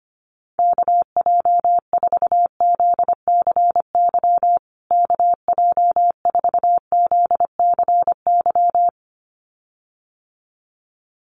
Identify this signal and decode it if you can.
Morse code, heard as a keyed tone, about 25 wpm, 700 Hz. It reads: KJ4ZCY KJ4ZCY
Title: USA Call Signs Repeated 2x 25wpm